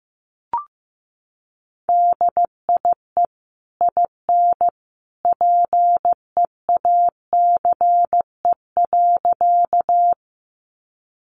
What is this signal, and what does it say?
Morse code, heard as a keyed tone, about 15 wpm, 700 Hz. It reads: DIE IN PEACE.
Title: Sentences from Top 1500 Words 15wpm